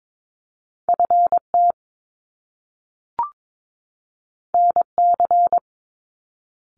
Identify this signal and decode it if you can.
Morse code, heard as a keyed tone, about 22 wpm, 700 Hz. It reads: FT NC
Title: Top Two Letter Combinations in Words 22wpm